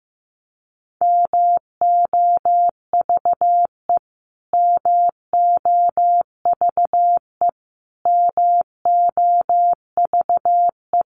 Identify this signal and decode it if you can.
Morse code, heard as a keyed tone, about 14 wpm, 700 Hz. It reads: MOVE MOVE MOVE
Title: Top 200 Words Repeated 15wpm